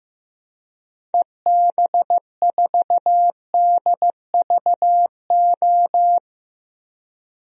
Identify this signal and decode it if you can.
Morse code, heard as a keyed tone, about 15 wpm, 700 Hz. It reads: EB4DVO
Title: DXCC-100 Call Signs - 6 Characters - 15wpm